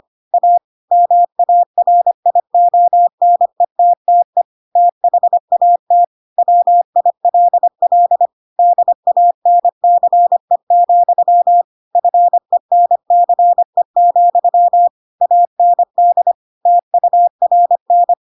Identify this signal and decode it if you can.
Morse code, heard as a keyed tone, about 25 wpm, 700 Hz. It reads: A MARIONETTE THAT WILL DANCE, FENCE, AND TURN